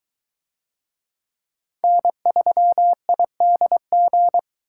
Morse code, heard as a keyed tone, about 22 wpm, 700 Hz. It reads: N3IDG